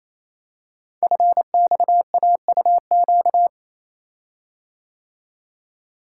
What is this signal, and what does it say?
Morse code, heard as a keyed tone, about 27 wpm, 700 Hz. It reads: FXAUQ
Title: Code Groups - 5 Letters - 28wpm